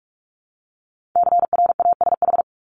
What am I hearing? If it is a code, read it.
Morse code, heard as a keyed tone, about 46 wpm, 700 Hz. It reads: CRASH
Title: Multi-letter Phonemes Repeated 3x 45wpm